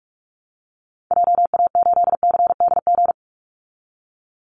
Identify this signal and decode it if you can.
Morse code, heard as a keyed tone, about 44 wpm, 700 Hz. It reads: JA8CDG
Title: Call Signs Repeated 3x - 45wpm